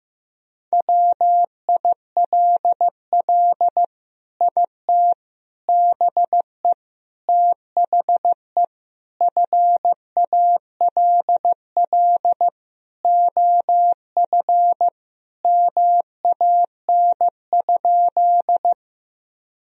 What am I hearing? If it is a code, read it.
Morse code, heard as a keyed tone, about 15 wpm, 700 Hz. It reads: WILL IT BE THE FALL OF MAN?
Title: Sentences from Top 1000 Words - Review Part 2 - 15wpm